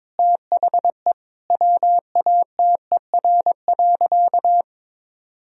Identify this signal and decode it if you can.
Morse code, heard as a keyed tone, about 21 wpm, 700 Hz. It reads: THE WATER.
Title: Sentences from Top 1000 Words - Review Part 1 - 22wpm